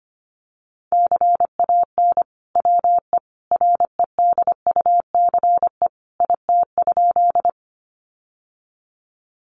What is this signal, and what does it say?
Morse code, heard as a keyed tone, about 25 wpm, 700 Hz. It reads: CAN WE REDUCE IT?
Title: Sentences from Top 1100 Words 25wpm